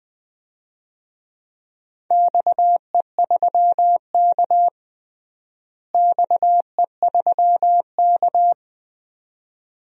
Morse code, heard as a keyed tone, about 21 wpm, 700 Hz. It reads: XE3K XE3K
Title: DXCC-100 Call Signs - 4 Characters - Repeated 2x - 20wpm